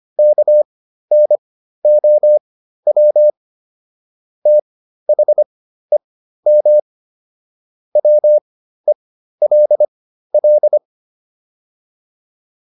Morse code, heard as a keyed tone, about 25 wpm, spaced as for 13 wpm, 600 Hz. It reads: KNOW THEM WELL